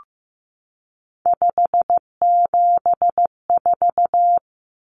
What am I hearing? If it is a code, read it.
Morse code, heard as a keyed tone, about 15 wpm, 700 Hz. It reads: 574
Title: Numbers - 3-Digit 15wpm